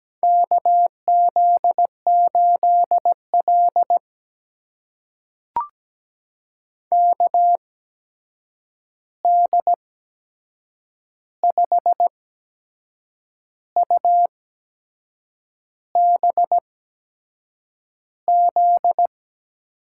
Morse code, heard as a keyed tone, about 17 wpm, spaced as for 6 wpm, 700 Hz. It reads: KZ8L KD5UBZ
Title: USA Call Signs - 8x Character Spacing - 17wpm